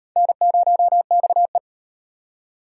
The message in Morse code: N0XE

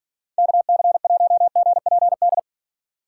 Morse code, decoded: KC1GPD

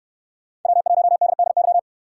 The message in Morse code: W1DDY

